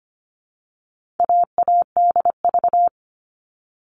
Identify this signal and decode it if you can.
Morse code, heard as a keyed tone, about 25 wpm, 700 Hz. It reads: AADV
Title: All Cut Numbers - 4-Digit 25wpm